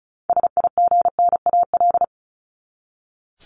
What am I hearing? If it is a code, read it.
Morse code, heard as a keyed tone, about 34 wpm, 700 Hz. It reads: SIGNAL
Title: Top 1251-1500 Words 35wpm